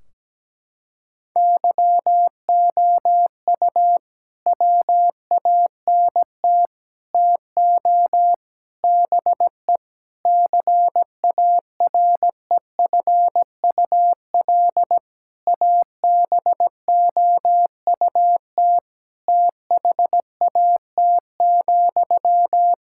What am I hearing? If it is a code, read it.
Morse code, heard as a keyed tone, about 17 wpm, 700 Hz. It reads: YOU WANT TO BE CAREFUL ABOUT THAT,
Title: Sentences from Top 1600 Words 17wpm